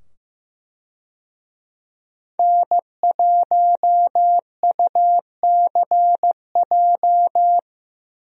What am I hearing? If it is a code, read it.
Morse code, heard as a keyed tone, about 16 wpm, 700 Hz. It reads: N1UCJ
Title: USA Call Signs - 5 Characters - 15wpm